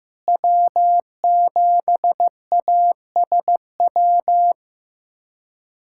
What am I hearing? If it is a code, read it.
Morse code, heard as a keyed tone, about 16 wpm, 700 Hz. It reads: W7ASW
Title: USA Callsign DE Callsign - 5 Characters - 15wpm